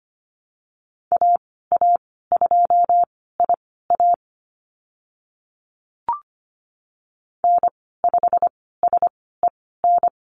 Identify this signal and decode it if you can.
Morse code, heard as a keyed tone, about 25 wpm, spaced as for 16 wpm, 700 Hz. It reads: AA2IA N5SEN
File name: Lesson 15g - Number 5 - Call Signs 25wpm